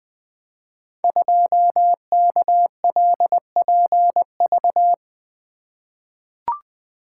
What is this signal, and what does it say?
Morse code, heard as a keyed tone, about 20 wpm, 700 Hz. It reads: 2KLPV
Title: Code Groups - 5 Letter:Numbers - 20wpm